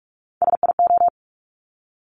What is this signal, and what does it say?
Morse code, heard as a keyed tone, about 46 wpm, 700 Hz. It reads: SIO
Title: Top Three Letter Combinations in Words 45wpm